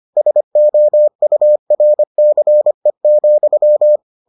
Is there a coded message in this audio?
Yes: Morse code from 0.2 to 4.0 s, heard as a keyed tone, about 25 wpm, 600 Hz. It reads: SOURCE,